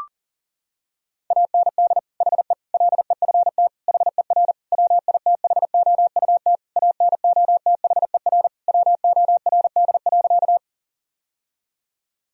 Morse code, heard as a keyed tone, about 40 wpm, 700 Hz. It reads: AND HE LEFT HER WITHOUT ANOTHER WORD.
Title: Sentences from Top 1000 Words - Review Part 3 - 40wpm